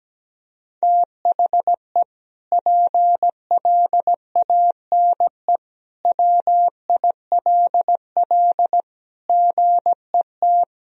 Morse code, heard as a keyed tone, about 17 wpm, 700 Hz. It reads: THE PLANE WILL GET
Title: Sentences from Top 1000 Words - Review Part 3 - 17wpm